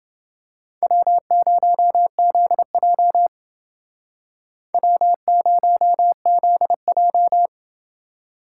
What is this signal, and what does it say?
Morse code, heard as a keyed tone, about 28 wpm, 700 Hz. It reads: W0ZJ W0ZJ